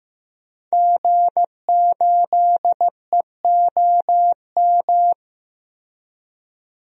Morse code, heard as a keyed tone, about 15 wpm, 700 Hz. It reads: G8EOM
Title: DXCC-100 Call Signs - 5 Characters - Repeated 2x - 15wpm